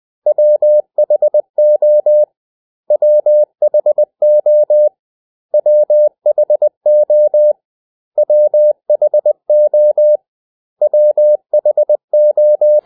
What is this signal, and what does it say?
Morse code, heard as a keyed tone, about 20 wpm, 600 Hz. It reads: WHO WHO WHO WHO WHO
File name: cw / 100 Common English Words